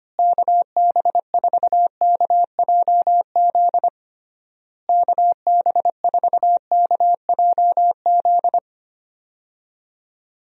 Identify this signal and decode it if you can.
Morse code, heard as a keyed tone, about 25 wpm, 700 Hz. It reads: KB4KJZ KB4KJZ
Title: USA Call Signs - 6 Characters - Repeated 3x - 25wpm